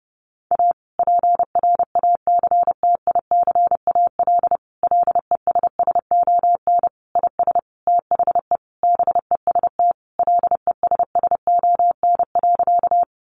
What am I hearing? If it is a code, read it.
Morse code, heard as a keyed tone, about 30 wpm, 700 Hz. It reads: A PRACTICAL LESSON IS THE BEST LESSON.
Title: Sentences from Top 1400 Words 30wpm